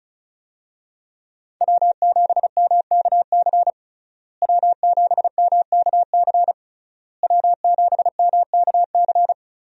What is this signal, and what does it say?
Morse code, heard as a keyed tone, about 36 wpm, 700 Hz. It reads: W7MKC W7MKC W7MKC
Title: USA Call Signs - 5 Characters - Repeated 3x - 35wpm